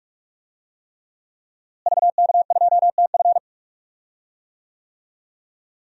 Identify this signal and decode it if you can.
Morse code, heard as a keyed tone, about 47 wpm, 700 Hz. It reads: VK2TF
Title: DXCC-100 Call Signs - 5 Characters - Repeated 3x - 45wpm